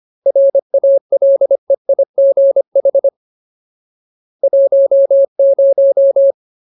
Morse code, heard as a keyed tone, about 25 wpm, 550 Hz. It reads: RALEIGH 10
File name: Ultimate Word List / 25WPM Ultimate